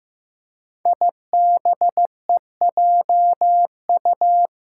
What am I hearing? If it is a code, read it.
Morse code, heard as a keyed tone, about 15 wpm, 700 Hz. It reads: IBEJU
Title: Five Challenging Letter-Number Combinations 15wpm